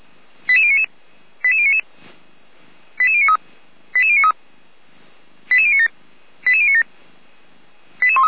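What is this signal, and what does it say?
Five-tone selective call (ZVEI), heard as quick runs of five tones, tones of about 70 ms. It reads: 88099 88099 88093 88093 88097 88097
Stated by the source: ZVEI five-tone selective call 88199 received on an SDR by Feuerwehr Orlamuende.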